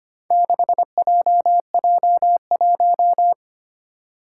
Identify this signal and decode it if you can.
Morse code, heard as a keyed tone, about 25 wpm, 700 Hz. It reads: 6JJ1
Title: Four Challenging Letter-Number Combinations 25wpm